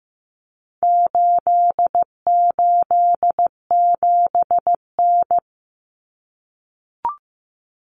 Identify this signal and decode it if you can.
Morse code, heard as a keyed tone, about 15 wpm, 700 Hz. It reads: 887N
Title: Common Cut Numbers - 4-Digit 15wpm